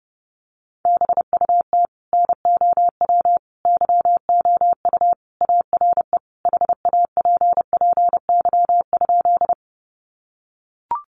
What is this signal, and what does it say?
Morse code, heard as a keyed tone, about 28 wpm, 700 Hz. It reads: BUT NOW YOU ARE HAPPY?
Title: Sentences from Top 1000 Words 30wpm